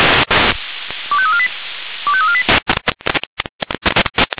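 Five-tone selective call (ZVEI), heard as quick runs of five tones, tones of about 70 ms. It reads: 25348 25348
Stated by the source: ZVEI five-tone selective call received on an SDR.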